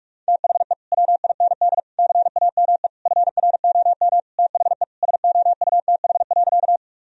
Morse code, heard as a keyed tone, about 46 wpm, 700 Hz. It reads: THE WIND CAME FROM THE SOUTH.
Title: Sentences from Top 1000 Words - Review Part 2 - 45wpm